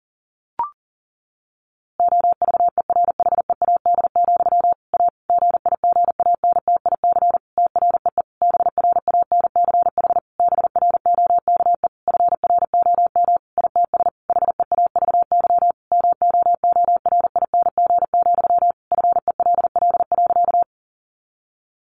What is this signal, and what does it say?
Morse code, heard as a keyed tone, about 37 wpm, 700 Hz. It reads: OVERHEAD, A GIGANTIC TREE BRANCH BROKE FROM ITS HEAVY MOORING, FELL.
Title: A Sound Of Thunder 40wpm